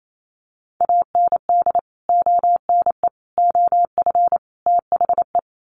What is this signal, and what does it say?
Morse code, heard as a keyed tone, about 28 wpm, 700 Hz. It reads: AND ONE OF THE
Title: Sets of 4 Words 28wpm